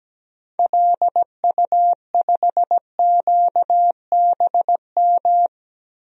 Morse code, heard as a keyed tone, about 17 wpm, 700 Hz. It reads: LU5QBM
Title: DXCC-100 Call Signs - 6 Characters - Repeated 2x - 17wpm